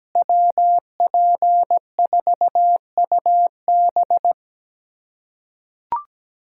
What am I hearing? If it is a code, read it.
Morse code, heard as a keyed tone, about 17 wpm, 700 Hz. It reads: WP4UB
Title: USA Callsign DE Callsign - 5 Characters - 17wpm